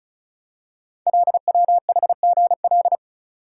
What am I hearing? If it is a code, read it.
Morse code, heard as a keyed tone, about 36 wpm, 700 Hz. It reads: LWHGL
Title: Code Groups - 5 Letter:Numbers - 35wpm